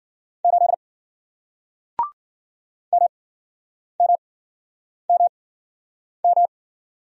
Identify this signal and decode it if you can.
Morse code, heard as a keyed tone, about 51 wpm, spaced as for 9 wpm, 700 Hz. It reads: 8 MMMM